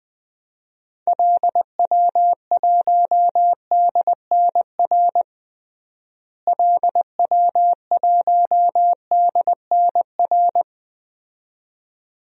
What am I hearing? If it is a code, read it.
Morse code, heard as a keyed tone, about 20 wpm, 700 Hz. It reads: LW1DNR LW1DNR
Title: DXCC-100 Call Signs - 6 Characters - Repeated 2x - 20wpm